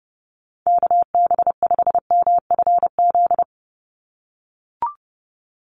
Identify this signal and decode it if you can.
Morse code, heard as a keyed tone, about 30 wpm, 700 Hz. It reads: KB5MFZ
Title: USA Call Signs 30wpm